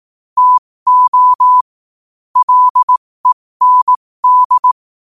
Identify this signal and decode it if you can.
Morse code, heard as a keyed tone, about 18 wpm, spaced as for 15 wpm, 1000 Hz. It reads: TO LEND